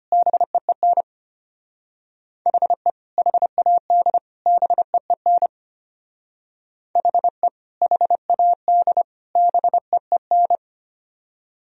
Morse code, heard as a keyed tone, about 31 wpm, 700 Hz. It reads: BEEN HE HAD BEEN HE HAD BEEN
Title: Speed-Racing - Sets of 3 Words 25wpm